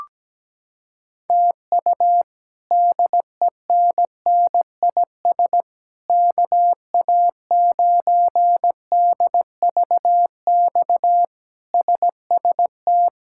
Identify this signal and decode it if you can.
Morse code, heard as a keyed tone, about 17 wpm, 700 Hz. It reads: TU DENNIS KA9DVX SST
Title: K1USN SST QSO Practice - 17wpm